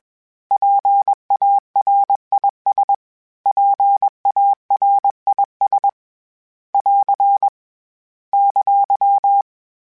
Morse code, heard as a keyed tone, about 21 wpm, 800 Hz. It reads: PARIS PARIS <AR> !
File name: PARIS Calibration / PARIS mp3s / PARIS@20